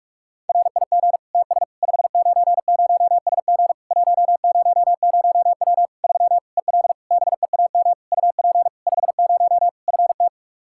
Morse code, heard as a keyed tone, about 42 wpm, 700 Hz. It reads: RIG TS 590SG 100W 3 EL BEAM UP 50 FT
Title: K1USN SST QSO Practice 45wpm